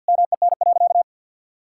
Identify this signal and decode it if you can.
Morse code, heard as a keyed tone, about 47 wpm, 700 Hz. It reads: MEN.
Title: Sentences from Top 1000 Words - Review Part 6 - 50wpm